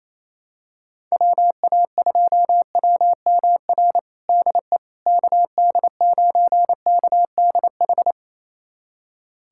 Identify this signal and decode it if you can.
Morse code, heard as a keyed tone, about 28 wpm, 700 Hz. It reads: WA2WMR DE KD9KDH
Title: USA Callsign DE Callsign - 6 Characters - 28wpm